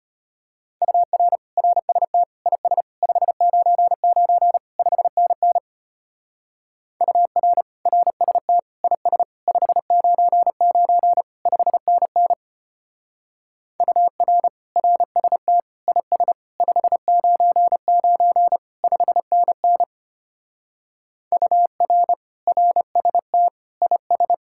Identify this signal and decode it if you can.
Morse code, heard as a keyed tone, about 32 wpm, 700 Hz. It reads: UR RST IS 599 5NN UR RST IS 599 5NN UR RST IS 599 5NN UR RST IS